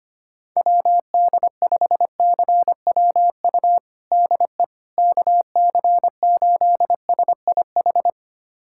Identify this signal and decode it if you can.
Morse code, heard as a keyed tone, about 25 wpm, 700 Hz. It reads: WD5CWU DE KC8SIH
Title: USA Callsign DE Callsign 25wpm